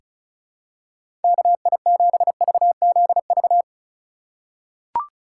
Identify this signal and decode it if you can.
Morse code, heard as a keyed tone, about 34 wpm, 700 Hz. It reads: KI7VZV
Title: USA Call Signs - 2x Character Spacing - 35wpm